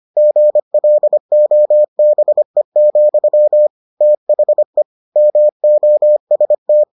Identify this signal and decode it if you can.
Morse code, heard as a keyed tone, about 25 wpm, 600 Hz. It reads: GLOBE, THE MOST